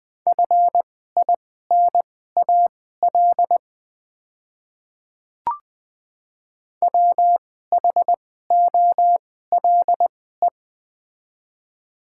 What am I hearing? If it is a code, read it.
Morse code, heard as a keyed tone, about 20 wpm, spaced as for 14 wpm, 700 Hz. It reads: FINAL WHOLE
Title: 2x Character Spacing - Top 500 Words - 20wpm